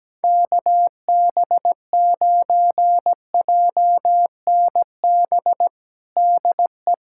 Morse code, heard as a keyed tone, about 17 wpm, 700 Hz. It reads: KB9JNB DE
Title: USA Callsign DE Callsign 17wpm